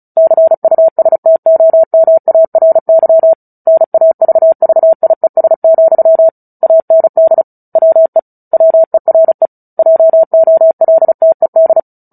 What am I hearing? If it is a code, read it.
Morse code, heard as a keyed tone, about 34 wpm, 650 Hz. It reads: CUSTOMARY NAVVIES, AND WE WERE JOLTED
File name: Book Two / 09-Chapter 9 - 35 WPM